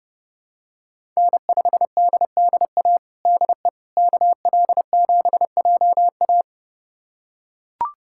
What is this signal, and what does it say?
Morse code, heard as a keyed tone, about 32 wpm, 700 Hz. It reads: N5DDA DE KL7JA
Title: USA Callsign DE Callsign - 5 Characters - 30wpm